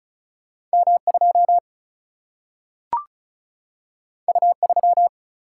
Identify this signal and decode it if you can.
Morse code, heard as a keyed tone, about 35 wpm, 700 Hz. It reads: M2 U3
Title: Single-Letter Single-Number 35wpm